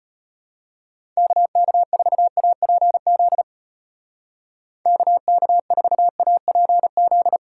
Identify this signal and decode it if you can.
Morse code, heard as a keyed tone, about 31 wpm, 700 Hz. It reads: KK4APZ KK4APZ